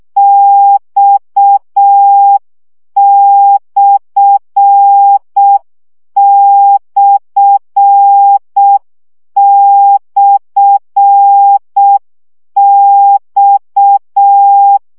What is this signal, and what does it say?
Morse code, heard as a keyed tone, about 6 wpm, 800 Hz. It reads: X///X
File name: CW gyakorlas x per 5 baud